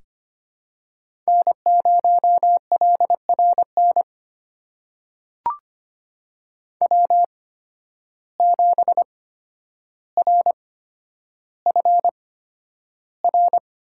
Morse code, heard as a keyed tone, about 25 wpm, spaced as for 11 wpm, 700 Hz. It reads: N0LRN W7RFR
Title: USA Call Signs - 5 Characters - 8x Character Spacing - 25wpm